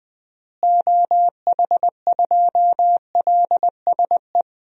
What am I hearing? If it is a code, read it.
Morse code, heard as a keyed tone, about 20 wpm, 700 Hz. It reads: OH2LSE
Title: DXCC-100 Call Signs - 6 Characters - Repeated 3x - 20wpm